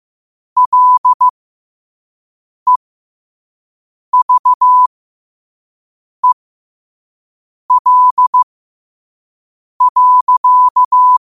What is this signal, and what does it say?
Morse code, heard as a keyed tone, about 15 wpm, spaced as for 5 wpm, 1000 Hz. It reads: LEVEL.